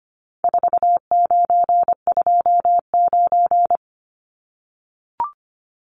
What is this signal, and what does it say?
Morse code, heard as a keyed tone, about 25 wpm, 700 Hz. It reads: 4929
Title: Numbers - 4-Digit 25wpm